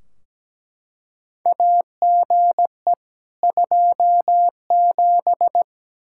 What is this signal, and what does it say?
Morse code, heard as a keyed tone, about 17 wpm, 700 Hz. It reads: AGE 27
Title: K1USN SST QSO Practice 17wpm